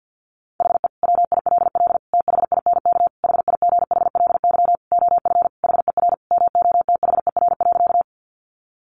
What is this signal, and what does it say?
Morse code, heard as a keyed tone, about 50 wpm, 700 Hz. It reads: HE WILL THINK HIGHLY OF HER MOTHER.
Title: Sentences from Top 1100 Words 50wpm